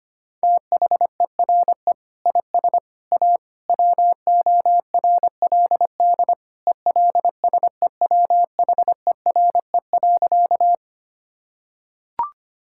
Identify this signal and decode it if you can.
Morse code, heard as a keyed tone, about 25 wpm, 700 Hz. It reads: THERE IS A WORLD ELSEWHERE.
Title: Sentences from Top 1600 Words 25wpm